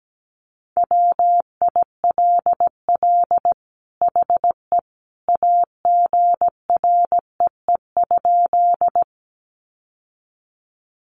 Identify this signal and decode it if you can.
Morse code, heard as a keyed tone, about 17 wpm, 700 Hz. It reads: WILL HE AGREE?
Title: Sentences from Top 1100 Words 17wpm